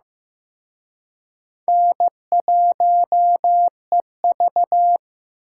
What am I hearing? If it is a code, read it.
Morse code, heard as a keyed tone, about 18 wpm, 700 Hz. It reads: N1EV
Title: Speed Racing - USA 1x2 Call Signs 15wpm